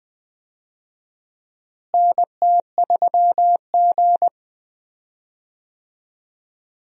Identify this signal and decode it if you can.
Morse code, heard as a keyed tone, about 20 wpm, 700 Hz. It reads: NT3G
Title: USA Call Signs - 4 Characters - Repeated 3x - 20wpm